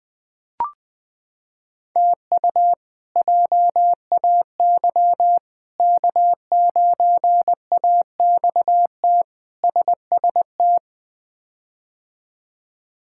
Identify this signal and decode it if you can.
Morse code, heard as a keyed tone, about 21 wpm, 700 Hz. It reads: TU JAY K9AXT SST
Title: K1USN SST QSO Practice - 20wpm